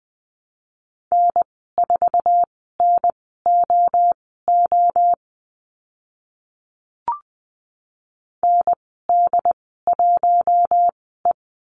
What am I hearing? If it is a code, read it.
Morse code, heard as a keyed tone, about 20 wpm, spaced as for 14 wpm, 700 Hz. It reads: N4NOO ND1E
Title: 069 Lesson 12g - Letter d - Call Signs 20wpm